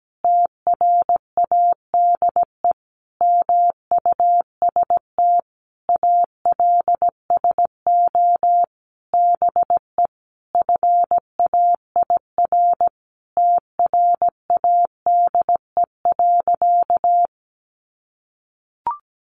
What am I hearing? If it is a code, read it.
Morse code, heard as a keyed tone, about 17 wpm, 700 Hz. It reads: TRADE MUST ALSO BE FAIR TRADE.
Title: Sentences from Top 1300 Words 17wpm